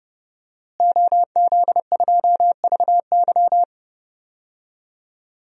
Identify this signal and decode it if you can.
Morse code, heard as a keyed tone, about 30 wpm, 700 Hz. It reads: OZ2VY
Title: DXCC-100 Call Signs - 5 Characters - Repeated 3x - 30wpm